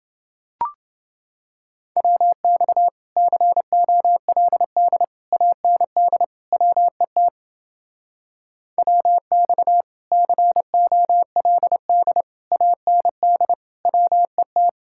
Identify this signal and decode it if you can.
Morse code, heard as a keyed tone, about 30 wpm, 700 Hz. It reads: WX COLD AND WET WX COLD AND WET